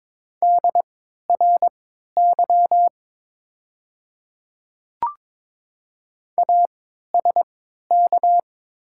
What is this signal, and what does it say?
Morse code, heard as a keyed tone, about 22 wpm, spaced as for 12 wpm, 700 Hz. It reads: DRY ASK
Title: 3x Character Spacing - Top 500 Words - 22wpm